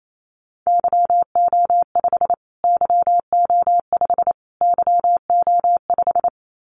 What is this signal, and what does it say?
Morse code, heard as a keyed tone, about 28 wpm, 700 Hz. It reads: YO5 YO5 YO5
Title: Callsign Prefixes for Top 25 DXCC - Repeated 3x - 28wpm